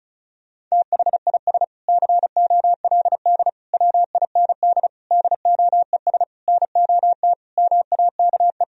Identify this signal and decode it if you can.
Morse code, heard as a keyed tone, about 36 wpm, 700 Hz. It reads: THIS COLD WIND DOES NOT MAKE
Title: Sentences from Top 1000 Words - Review Part 2 - 35wpm